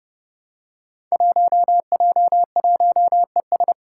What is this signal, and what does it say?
Morse code, heard as a keyed tone, about 32 wpm, 700 Hz. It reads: 1J1ES
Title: Five Challenging Letter-Number Combinations 30wpm